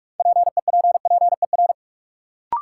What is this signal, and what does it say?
Morse code, heard as a keyed tone, about 44 wpm, 700 Hz. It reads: PEPPER
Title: Sets of 5 Words - New Word Drill - 45wpm